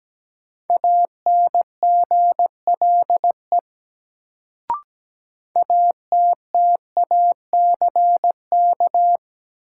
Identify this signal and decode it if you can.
Morse code, heard as a keyed tone, about 17 wpm, 700 Hz. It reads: ANGLE ATTACK
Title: Sentences from Top 1500 Words - New Word Drill Alt - 17wpm